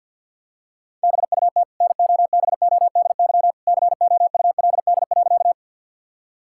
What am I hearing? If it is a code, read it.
Morse code, heard as a keyed tone, about 50 wpm, 700 Hz. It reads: BUT NOBODY COULD.
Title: Sentences from Top 1600 Words 50wpm